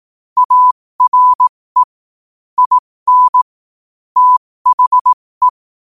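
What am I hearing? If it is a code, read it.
Morse code, heard as a keyed tone, about 18 wpm, spaced as for 15 wpm, 1000 Hz. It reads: ARE IN THE
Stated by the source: Morse practice sent at 15 wpm.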